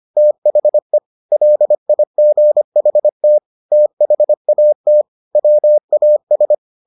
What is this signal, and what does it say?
Morse code, heard as a keyed tone, about 25 wpm, 600 Hz. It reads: THE LIGHT THAT WAS